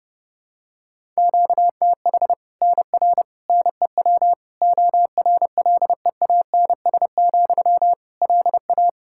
Morse code, heard as a keyed tone, about 31 wpm, 700 Hz. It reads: QTH NR NEW ORLEANS, LA
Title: Practice QSO - Exchange 1 - Beginning - 30wpm